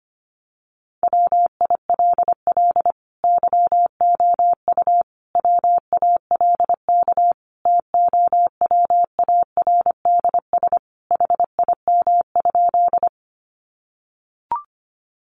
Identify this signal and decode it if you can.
Morse code, heard as a keyed tone, about 25 wpm, 700 Hz. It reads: WILL YOU WALK TOWARDS HIM?
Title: Sentences from Top 1100 Words 25wpm